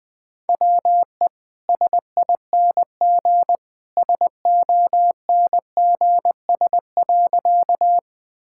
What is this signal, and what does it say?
Morse code, heard as a keyed tone, about 20 wpm, 700 Hz. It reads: WE SING SONGS.
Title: Easy Sentences with Hints - 20wpm